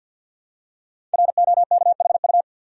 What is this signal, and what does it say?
Morse code, heard as a keyed tone, about 53 wpm, 700 Hz. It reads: ROKHU
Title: Code Groups - 5 Letters - 50wpm